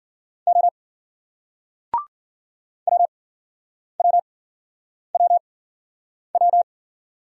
Morse code, heard as a keyed tone, about 56 wpm, spaced as for 9 wpm, 700 Hz. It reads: O WWWW